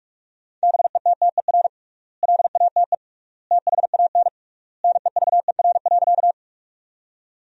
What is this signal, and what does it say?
Morse code, heard as a keyed tone, about 46 wpm, 700 Hz. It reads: BETTER LATE THAN NEVER.
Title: 3x Word-Spacing - Sentences from Top 400 Words - 45wpm